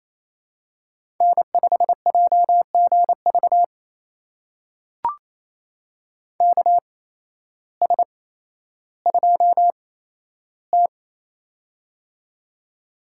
Morse code, heard as a keyed tone, about 28 wpm, spaced as for 13 wpm, 700 Hz. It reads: N5JGV KS2T
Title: USA Call Signs - 8x Character Spacing - 28wpm